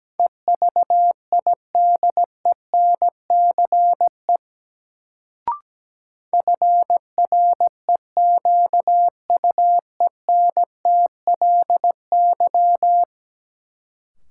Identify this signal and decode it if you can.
Morse code, heard as a keyed tone, about 17 wpm, 700 Hz. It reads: EVIDENCE FREQUENTLY
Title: Sentences from Top 1300 Words - New Word Drill Alt - 17wpm